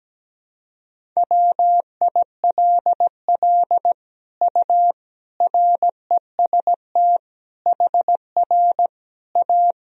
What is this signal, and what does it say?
Morse code, heard as a keyed tone, about 17 wpm, 700 Hz. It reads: WILL U REST HR A
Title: Sentences from Top 500 Words and Abbreviations 17wpm